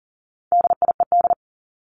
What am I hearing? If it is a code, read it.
Morse code, heard as a keyed tone, about 40 wpm, 700 Hz. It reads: DIED
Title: ED Suffix - Top 100 Words with Base Words 40wpm